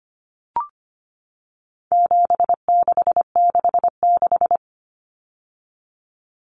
Morse code, heard as a keyed tone, about 25 wpm, 700 Hz. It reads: 7666
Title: Numbers - 4-Digit 25wpm